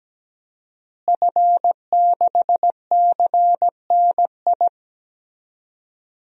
Morse code, heard as a keyed tone, about 17 wpm, 700 Hz. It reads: F6CNI
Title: DXCC-100 Call Signs - 5 Characters - Repeated 2x - 17wpm